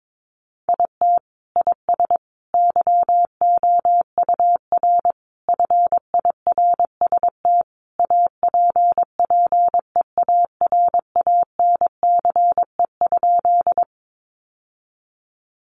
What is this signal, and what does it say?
Morse code, heard as a keyed tone, about 22 wpm, 700 Hz. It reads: IT IS YOUR FIRST APPEARANCE?
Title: Sentences from Top 1100 Words 22wpm